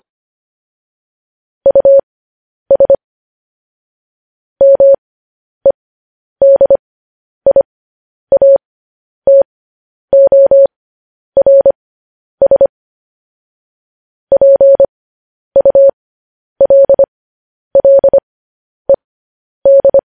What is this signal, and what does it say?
Morse code, heard as a keyed tone, about 25 wpm, spaced as for 10 wpm, 550 Hz. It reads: US MEDIATORS PULLED
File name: podcast / News headlines-1x-25x10